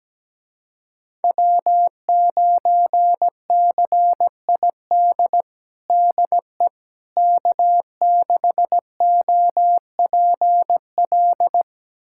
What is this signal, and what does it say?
Morse code, heard as a keyed tone, about 17 wpm, 700 Hz. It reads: W9CID DE K6OPL
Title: USA Callsign DE Callsign - 5 Characters - 17wpm